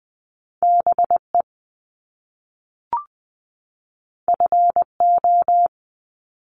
Morse code, heard as a keyed tone, about 20 wpm, 700 Hz. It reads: BE FO
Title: Top Two Letter Combinations in Words 20wpm